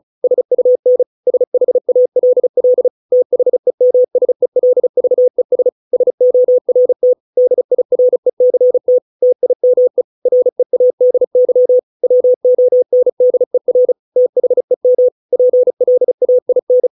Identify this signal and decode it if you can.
Morse code, heard as a keyed tone, about 34 wpm, 500 Hz. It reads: SUN SHALL THEMSELVES SORT DIRECT TIME READY WONDER THEM PLAIN